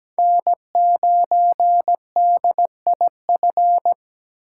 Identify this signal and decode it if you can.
Morse code, heard as a keyed tone, about 17 wpm, 700 Hz. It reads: N9DIF
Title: USA Call Signs - 5 Characters - 6x Character Spacing - 17wpm